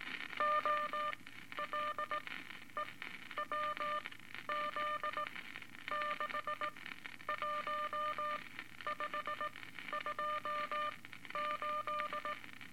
Morse code, heard as a keyed tone, about 18 wpm, spaced as for 11 wpm, 1200 Hz, with noise. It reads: OLEWZ 61528
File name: Eight Words Per Minute